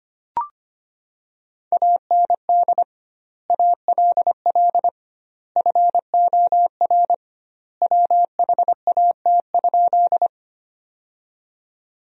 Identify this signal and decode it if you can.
Morse code, heard as a keyed tone, about 25 wpm, 700 Hz. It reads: AND ALL FOR WHAT?
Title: 2x Word-Spacing - Sentences from Top 100 Words - 25wpm